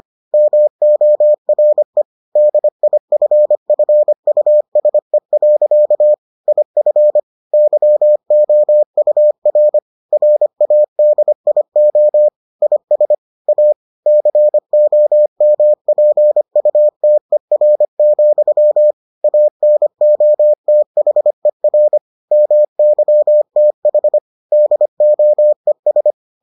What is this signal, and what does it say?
Morse code, heard as a keyed tone, about 25 wpm, 600 Hz. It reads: MORE DIFFUSE. IF YOUR RADIO IS A COMPUTER, ANOTHER MYTH DOES